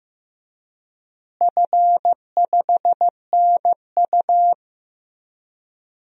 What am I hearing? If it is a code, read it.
Morse code, heard as a keyed tone, about 15 wpm, 700 Hz. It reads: F5NU
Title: DXCC-100 Call Signs - 4 Characters - Repeated 2x - 15wpm